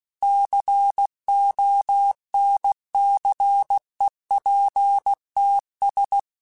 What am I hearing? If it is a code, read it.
Morse code, heard as a keyed tone, about 16 wpm, 800 Hz. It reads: CONCEPTS
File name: Technical Subjects / Technical Subjects-mp3s / Gates Intro @15